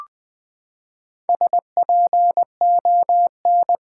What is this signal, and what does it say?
Morse code, heard as a keyed tone, about 20 wpm, 700 Hz. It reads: SPON
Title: Build A Word - 4 Letters At A Time - 11-Letter Words - 20wpm